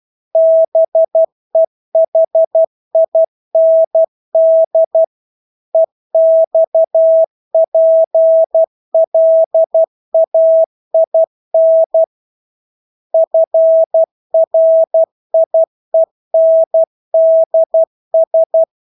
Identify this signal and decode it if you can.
Morse code, heard as a keyed tone, about 12 wpm, 650 Hz. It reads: BEHIND EXPLAIN FRIENDS
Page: Engelske ord 12 wpm